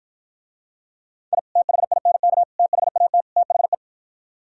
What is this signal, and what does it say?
Morse code, heard as a keyed tone, about 53 wpm, 700 Hz. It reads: I THINK THAT THE